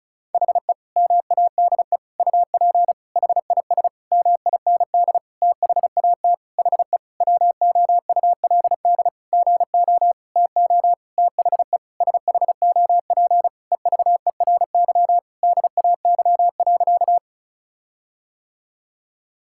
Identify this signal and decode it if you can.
Morse code, heard as a keyed tone, about 34 wpm, 700 Hz. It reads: HE MADE UP HIS MIND THAT HE WOULD GO TO THE SHOP EVERY DAY.
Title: Sentences from Top 1600 Words 35wpm